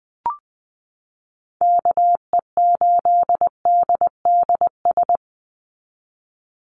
Morse code, heard as a keyed tone, about 20 wpm, 700 Hz. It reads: KE8DDS
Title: USA Call Signs 20wpm